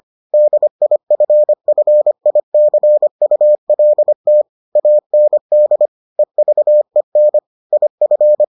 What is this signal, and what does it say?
Morse code, heard as a keyed tone, about 25 wpm, 600 Hz. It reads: DIFFICULT AND EVEN IF